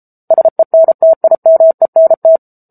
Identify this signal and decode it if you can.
Morse code, heard as a keyed tone, about 34 wpm, 650 Hz. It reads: SENTIMENT
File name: Book Two / 07-Chapter 7 - 33 WPM